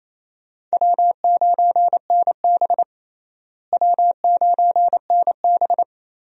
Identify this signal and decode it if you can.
Morse code, heard as a keyed tone, about 27 wpm, 700 Hz. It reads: W9NB W9NB
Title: USA Call Signs - 4 Characters - Repeated 2x - 28wpm